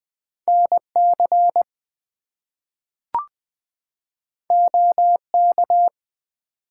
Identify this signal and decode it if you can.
Morse code, heard as a keyed tone, about 20 wpm, 700 Hz. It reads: NC OK
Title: US States and Canadian Provinces 20wpm